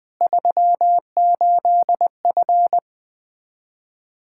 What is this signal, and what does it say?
Morse code, heard as a keyed tone, about 21 wpm, 700 Hz. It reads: 38F
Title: Practice QSO - Exchange 2 - Info - 20wpm